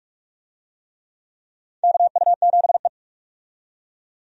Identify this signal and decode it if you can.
Morse code, heard as a keyed tone, about 46 wpm, 700 Hz. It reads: KU7E
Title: USA Call Signs - 4 Characters - 45wpm